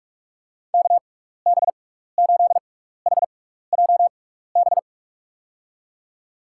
Morse code, heard as a keyed tone, about 46 wpm, spaced as for 15 wpm, 700 Hz. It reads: KB8HJB
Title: USA Call Signs - 6 Characters - 6x Character Spacing - 45wpm